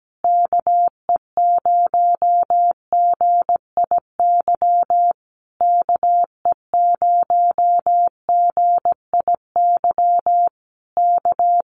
Morse code, heard as a keyed tone, about 17 wpm, 700 Hz. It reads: KE0GIY KE0GIY K
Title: Common QSO Phrases Encore 17wpm